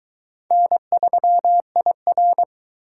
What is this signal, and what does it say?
Morse code, heard as a keyed tone, about 22 wpm, 700 Hz. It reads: N3IR